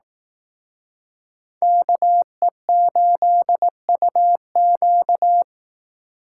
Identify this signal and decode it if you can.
Morse code, heard as a keyed tone, about 19 wpm, 700 Hz. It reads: KE8UQ
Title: Speed Racing - USA Call Signs 15wpm